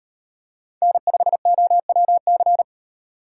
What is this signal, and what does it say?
Morse code, heard as a keyed tone, about 37 wpm, 700 Hz. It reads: N5OWC